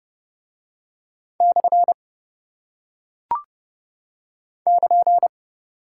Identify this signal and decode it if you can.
Morse code, heard as a keyed tone, about 28 wpm, 700 Hz. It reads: / <KN>